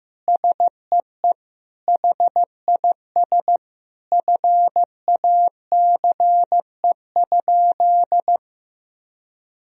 Morse code, heard as a keyed tone, about 15 wpm, 700 Hz. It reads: SEE HIS FACE?
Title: Sentences from Top 1000 Words - Review Part 2 - 15wpm